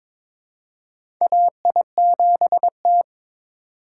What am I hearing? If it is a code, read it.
Morse code, heard as a keyed tone, about 22 wpm, 700 Hz. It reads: AI7T
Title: USA Call Signs - 4 Characters - 2x Character Spacing - 22wpm